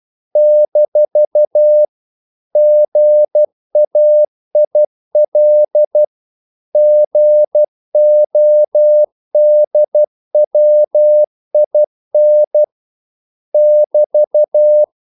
Morse code, heard as a keyed tone, about 12 wpm, 600 Hz. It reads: - GAIL GODWIN <BT>
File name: CW / Practice / EnglishQuotations0312